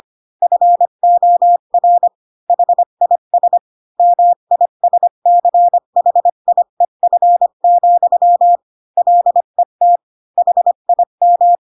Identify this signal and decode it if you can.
Morse code, heard as a keyed tone, about 25 wpm, 700 Hz. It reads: FOR HIS MISCHIEF, LET HIM
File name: Pinocchio / Pinocchio ch8 20 25 ik1hge